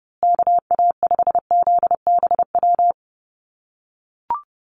Morse code, heard as a keyed tone, about 30 wpm, 700 Hz. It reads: KA5ZBW
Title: USA Call Signs Repeated 2x 30wpm